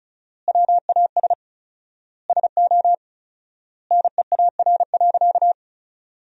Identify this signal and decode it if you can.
Morse code, heard as a keyed tone, about 34 wpm, 700 Hz. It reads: WAS SO NEAR.
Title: 4x Word-Spacing - Sentences from Top 300 Words - 35wpm